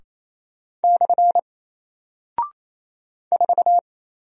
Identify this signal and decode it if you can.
Morse code, heard as a keyed tone, about 28 wpm, 700 Hz. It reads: / 4
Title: Single Character (Letter:Number:Punctuation:Prosign) Alt 28wpm